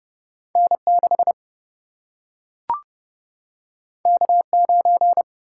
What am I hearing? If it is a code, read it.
Morse code, heard as a keyed tone, about 29 wpm, 700 Hz. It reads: N6 K9
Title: Single-Letter Single-Number 30wpm